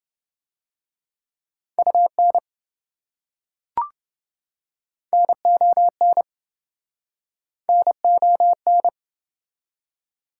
Morse code, heard as a keyed tone, about 26 wpm, 700 Hz. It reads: UN NON NON